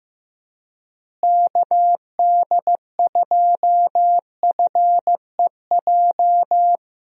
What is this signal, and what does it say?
Morse code, heard as a keyed tone, about 15 wpm, 700 Hz. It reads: KD2FEJ
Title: USA Call Signs - 6 Characters - Repeated 2x - 15wpm